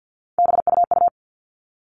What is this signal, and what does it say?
Morse code, heard as a keyed tone, about 50 wpm, 700 Hz. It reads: BUU
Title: All Cut Numbers - 3-Digit 50wpm